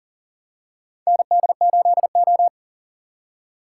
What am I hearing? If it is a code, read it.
Morse code, heard as a keyed tone, about 37 wpm, 700 Hz. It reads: ND8O
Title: USA Call Signs - 4 Characters - 3x Character Spacing - 40wpm